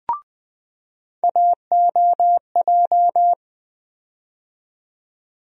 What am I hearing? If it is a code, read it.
Morse code, heard as a keyed tone, about 20 wpm, 700 Hz. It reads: AOJ
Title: USA Call Sign Postfixes - 3 Letters 20wpm